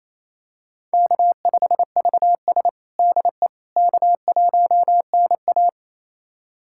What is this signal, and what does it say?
Morse code, heard as a keyed tone, about 28 wpm, 700 Hz. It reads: K5VS DE K1NA
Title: USA Callsign DE Callsign - 4 Characters - 28wpm